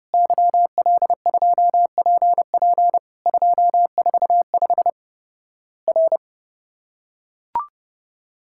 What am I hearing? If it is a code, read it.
Morse code, heard as a keyed tone, about 30 wpm, 700 Hz. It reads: YL2PP 245 R
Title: Worked All Europe DX Contest - QTC Exchange - 30wpm